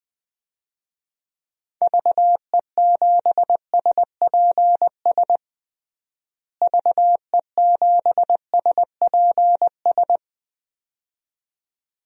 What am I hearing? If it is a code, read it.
Morse code, heard as a keyed tone, about 20 wpm, 700 Hz. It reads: VE7SPS VE7SPS
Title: DXCC-100 Call Signs - 6 Characters - Repeated 2x - 20wpm